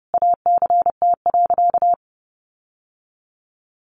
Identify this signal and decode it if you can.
Morse code, heard as a keyed tone, about 30 wpm, 700 Hz. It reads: ACT.
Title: Sentences from Top 1400 Words 30wpm